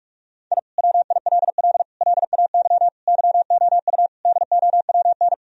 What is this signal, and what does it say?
Morse code, heard as a keyed tone, about 46 wpm, 700 Hz. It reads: I WILL LAY YOU DOWN
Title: Sentences from Top 1000 Words - Review Part 2 - 45wpm